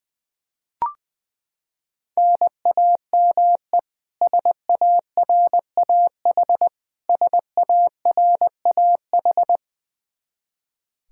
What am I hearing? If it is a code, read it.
Morse code, heard as a keyed tone, about 20 wpm, 700 Hz. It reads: NAME SARAH SARAH
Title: Practice QSO - Exchange 1 - Beginning - 20wpm